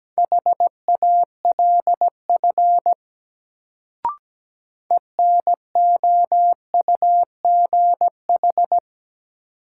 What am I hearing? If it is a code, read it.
Morse code, heard as a keyed tone, about 17 wpm, 700 Hz. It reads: HALF ENOUGH
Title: Sentences from Top 400 Words - New Word Drill - 17wpm